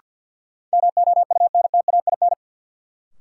Morse code, heard as a keyed tone, about 52 wpm, 700 Hz. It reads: MOUNTAIN
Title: Sentences from Top 1000 Words - Review Part 1 - New Word Drill - 50wpm